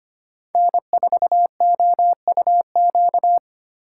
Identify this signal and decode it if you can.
Morse code, heard as a keyed tone, about 25 wpm, 700 Hz. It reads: N4OUQ
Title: USA Call Signs - 5 Characters - 4x Character Spacing - 25wpm